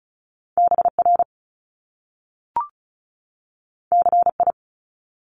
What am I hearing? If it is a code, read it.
Morse code, heard as a keyed tone, about 36 wpm, 700 Hz. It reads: BR CI
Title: Top Two Letter Combinations in Words 35wpm